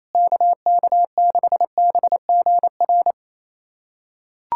Morse code, heard as a keyed tone, about 28 wpm, 700 Hz. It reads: KK6BGR
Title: USA Callsign DE Callsign - 6 Characters - 28wpm